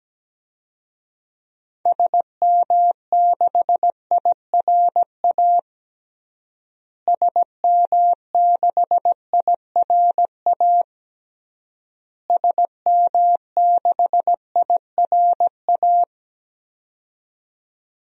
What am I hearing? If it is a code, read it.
Morse code, heard as a keyed tone, about 17 wpm, 700 Hz. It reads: SM6IRA SM6IRA SM6IRA
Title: DXCC-100 Call Signs - 6 Characters - Repeated 3x - 17wpm